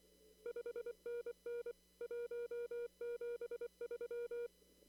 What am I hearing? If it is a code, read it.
Morse code, heard as a keyed tone, about 24 wpm, 480 Hz. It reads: 5NN 173
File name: CW / kt5j wpx cw-op-k5pi